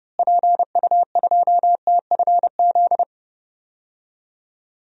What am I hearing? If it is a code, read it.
Morse code, heard as a keyed tone, about 29 wpm, 700 Hz. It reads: PU2TFZ
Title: DXCC-100 Call Signs - 6 Characters - Repeated 2x - 30wpm